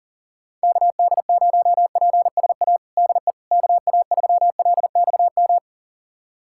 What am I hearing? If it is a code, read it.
Morse code, heard as a keyed tone, about 40 wpm, 700 Hz. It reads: KD0PSA DE KA3LXM
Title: USA Callsign DE Callsign - 6 Characters - 40wpm